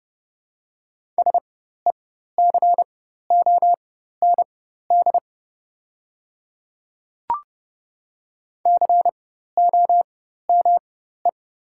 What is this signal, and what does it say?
Morse code, heard as a keyed tone, about 31 wpm, spaced as for 14 wpm, 700 Hz. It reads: SECOND COME
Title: 4x Character Spacing - Top 300 Words - 30wpm